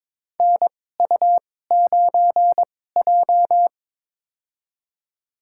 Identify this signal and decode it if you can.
Morse code, heard as a keyed tone, about 22 wpm, spaced as for 16 wpm, 700 Hz. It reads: NU9J
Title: USA Call Signs - 4 Characters - 2x Character Spacing - 22wpm